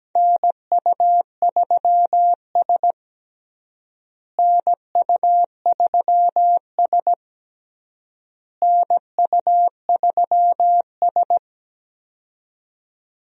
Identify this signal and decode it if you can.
Morse code, heard as a keyed tone, about 17 wpm, 700 Hz. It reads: NU3S NU3S NU3S
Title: USA Call Signs - 4 Characters - Repeated 3x - 17wpm